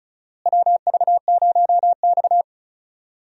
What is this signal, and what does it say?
Morse code, heard as a keyed tone, about 36 wpm, 700 Hz. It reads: WV0X
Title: USA Call Signs - 4 Characters - Repeated 2x - 35wpm